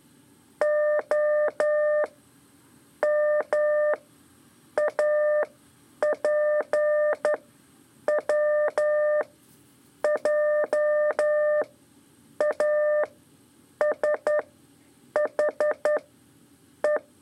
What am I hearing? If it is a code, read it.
Morse code, heard as a keyed tone, about 11 wpm, spaced as for 7 wpm, 600 Hz. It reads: OMAPWJASHE
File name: 3 lezione CW 2 AIR